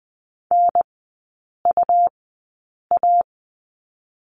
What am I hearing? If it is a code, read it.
Morse code, heard as a keyed tone, about 19 wpm, spaced as for 8 wpm, 700 Hz. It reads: NUA
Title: 121 Lesson 20e - Number 3 - Three Characters 20wpm